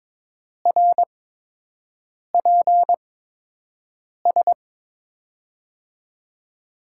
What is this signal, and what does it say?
Morse code, heard as a keyed tone, about 22 wpm, spaced as for 6 wpm, 700 Hz. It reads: RPS
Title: USA Call Signs - 6 Characters - 8x Character Spacing - 22wpm